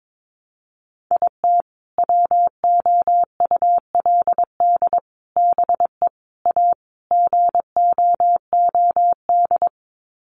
Morse code, heard as a keyed tone, about 22 wpm, 700 Hz. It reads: IT WOULD BE A GOOD
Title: Sets of 5 Words Encore2 22wpm